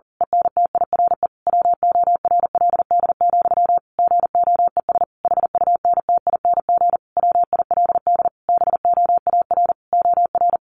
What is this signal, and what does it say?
Morse code, heard as a keyed tone, about 40 wpm, 700 Hz. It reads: ENTIRE WORLD, GOES HUNTING WILD BOAR OR
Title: A Sound Of Thunder 40wpm